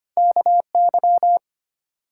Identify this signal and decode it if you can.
Morse code, heard as a keyed tone, about 25 wpm, 700 Hz. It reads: KY
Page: morse-code-ninja-practice-25wpm